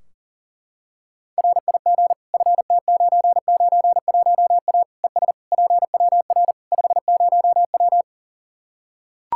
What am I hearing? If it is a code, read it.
Morse code, heard as a keyed tone, about 40 wpm, 700 Hz. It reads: RIG FT991A ES PWR 50W
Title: Practice QSO - Exchange 2 - Info - 40wpm